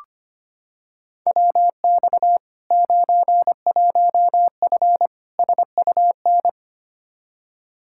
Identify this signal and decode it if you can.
Morse code, heard as a keyed tone, about 25 wpm, 700 Hz. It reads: WX 91F SUN
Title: Practice QSO - Exchange 2 - Info - 25wpm